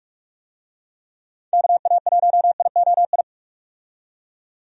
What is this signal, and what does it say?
Morse code, heard as a keyed tone, about 48 wpm, 700 Hz. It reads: KA1IOI
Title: USA Call Signs - 6 Characters - 45wpm